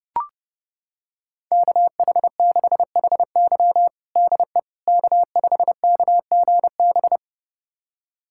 Morse code, heard as a keyed tone, about 32 wpm, 700 Hz. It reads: KH6HY DE K5KGB
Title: USA Callsign DE Callsign - 5 Characters - 30wpm